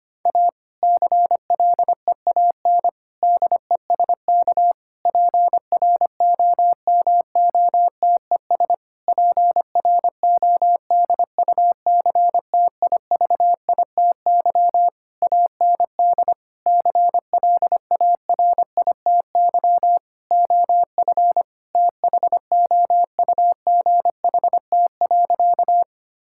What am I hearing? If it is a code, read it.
Morse code, heard as a keyed tone, about 25 wpm, 700 Hz. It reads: A CLEAN DESK PROMOTES PRODUCTIVITY AND CLARITY OF THOUGHT.
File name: practice / 25wpm / Difficult Sentences with Hints - 25wpm